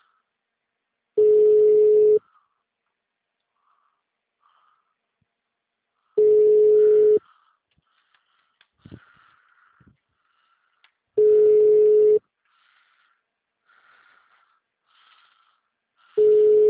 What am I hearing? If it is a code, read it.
Ringback tone (Europe, 425 Hz).
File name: out- -2007- - - .187